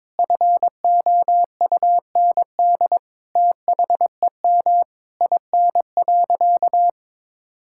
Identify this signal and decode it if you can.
Morse code, heard as a keyed tone, about 22 wpm, 700 Hz. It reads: FOUND THEM IN.
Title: Sentences from Top 1000 Words - Review Part 6 - 22wpm